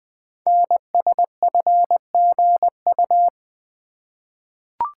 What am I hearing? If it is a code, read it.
Morse code, heard as a keyed tone, about 20 wpm, 700 Hz. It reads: NSFGU
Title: Code Groups - 5 Letters - 20wpm